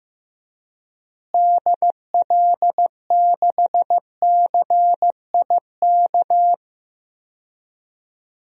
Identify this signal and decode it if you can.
Morse code, heard as a keyed tone, about 16 wpm, 700 Hz. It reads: DL6CIK
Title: DXCC-100 Call Signs - 6 Characters - Repeated 3x - 15wpm